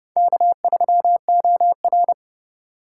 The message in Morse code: K3OR

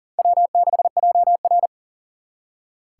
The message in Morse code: W6JR

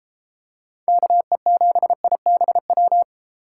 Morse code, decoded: KE7IBW

